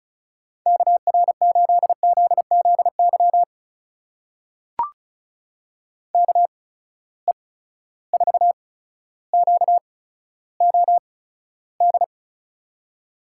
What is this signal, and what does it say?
Morse code, heard as a keyed tone, about 36 wpm, spaced as for 15 wpm, 700 Hz. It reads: KR8ZZY KE4QOD
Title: USA Call Signs - 6 Characters - 8x Character Spacing - 35wpm